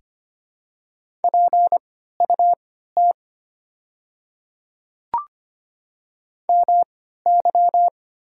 Morse code, heard as a keyed tone, about 25 wpm, spaced as for 14 wpm, 690 Hz. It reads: PUT MY
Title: 3x Character Spacing - Top 200 Words - 25wpm